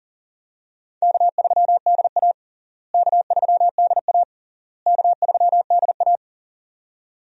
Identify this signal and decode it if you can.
Morse code, heard as a keyed tone, about 40 wpm, 700 Hz. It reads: K3DA K3DA K3DA
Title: USA Call Signs - 4 Characters - Repeated 3x - 40wpm